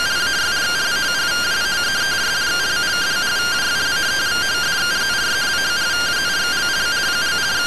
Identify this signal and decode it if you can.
Two-tone teleprinter signal, about 170 Hz shift.
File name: findings / MR CORONAL 55.gen-20 rtty rx